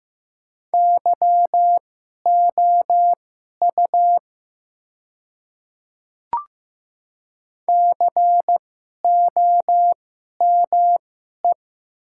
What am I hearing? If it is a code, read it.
Morse code, heard as a keyed tone, about 15 wpm, spaced as for 11 wpm, 700 Hz. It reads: YOU COME
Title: 2x Character Spacing - Top 100 Words - 15wpm